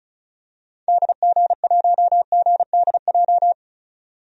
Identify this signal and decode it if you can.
Morse code, heard as a keyed tone, about 36 wpm, 700 Hz. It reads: DG1GDJ
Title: DXCC-100 Call Signs - 6 Characters - 35wpm